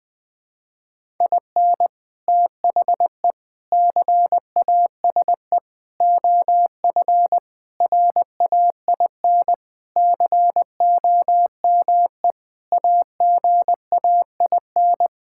Morse code, heard as a keyed tone, about 20 wpm, 700 Hz. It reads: IN THE CASE OF RAIN COME AGAIN
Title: Sentences with 4-Word Phrases - Spoken Beginning - 20wpm